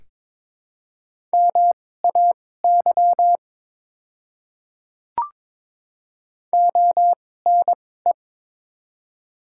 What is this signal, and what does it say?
Morse code, heard as a keyed tone, about 22 wpm, spaced as for 16 wpm, 700 Hz. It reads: MAY ONE
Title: 2x Character Spacing - Top 200 Words - 22wpm